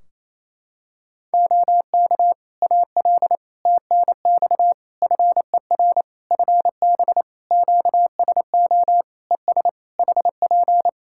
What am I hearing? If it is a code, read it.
Morse code, heard as a keyed tone, about 28 wpm, 700 Hz. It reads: OK AL TNX FER FB QSO ES HP
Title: Practice QSO - Exchange 4 - Ending - 28wpm